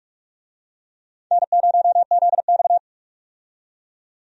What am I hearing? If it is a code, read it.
Morse code, heard as a keyed tone, about 47 wpm, 700 Hz. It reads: N0ZX
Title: USA Call Signs - 4 Characters - 45wpm